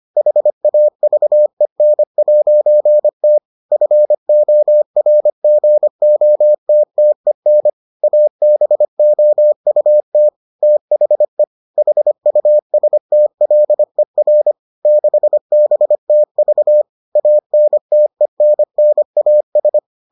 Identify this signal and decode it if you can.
Morse code, heard as a keyed tone, about 25 wpm, 600 Hz. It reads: HAVEN'T FORGOTTEN ABOUT THE HUSTLER 6BTV ANTENNAS